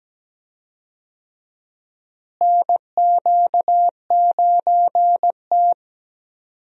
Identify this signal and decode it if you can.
Morse code, heard as a keyed tone, about 17 wpm, 700 Hz. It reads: NQ9T
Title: USA Call Signs - 4 Characters - Repeated 2x - 17wpm